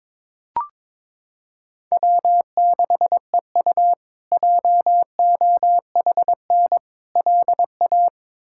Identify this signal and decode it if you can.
Morse code, heard as a keyed tone, about 22 wpm, 700 Hz. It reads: W6EU JOHN LA
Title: K1USN SST QSO Practice - 22wpm